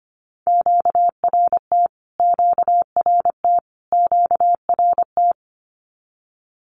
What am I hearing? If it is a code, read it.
Morse code, heard as a keyed tone, about 25 wpm, 700 Hz. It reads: QRT QRT QRT
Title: Q-Codes Repeated 3x 25wpm